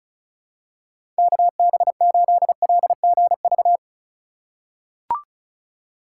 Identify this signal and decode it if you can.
Morse code, heard as a keyed tone, about 36 wpm, 700 Hz. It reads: KB8LGV
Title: USA Call Signs - 6 Characters - 3x Character Spacing - 35wpm